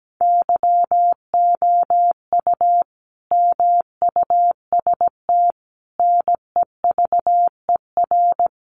Morse code, heard as a keyed tone, about 17 wpm, 700 Hz. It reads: YOU MUST NEVER
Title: Sentences from Top 300 Words 17wpm